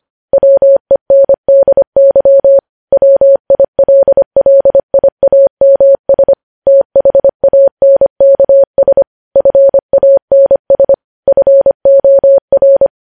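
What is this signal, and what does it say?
Morse code, heard as a keyed tone, about 25 wpm, 550 Hz. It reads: WENDY WILLIAMS THANKS FANS FOR
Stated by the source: Morse code practice of news headlines at 25 wpm, opening with vvv.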